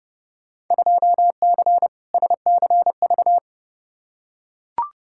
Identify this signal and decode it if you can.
Morse code, heard as a keyed tone, about 32 wpm, 700 Hz. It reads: 2C SCV
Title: Field Day Exchanges Encore - 30wpm